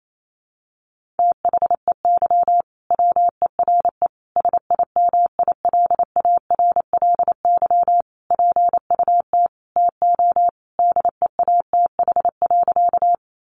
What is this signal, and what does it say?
Morse code, heard as a keyed tone, about 28 wpm, 700 Hz. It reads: THEY WERE SIMILARLY PUT TO DEATH.
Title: Sentences from Top 1400 Words 28wpm